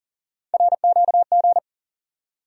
Morse code, heard as a keyed tone, about 40 wpm, 700 Hz. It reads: RQG
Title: USA Call Sign Postfixes - 3 Letters 40wpm